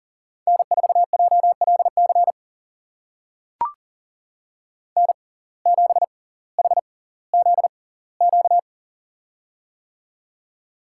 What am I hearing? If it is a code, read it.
Morse code, heard as a keyed tone, about 40 wpm, spaced as for 21 wpm, 690 Hz. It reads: N4JLC N7HZQ
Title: USA Call Signs - 5 Characters - 6x Character Spacing - 40wpm